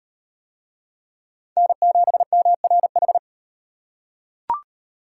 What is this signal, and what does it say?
Morse code, heard as a keyed tone, about 35 wpm, 700 Hz. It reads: N7MRH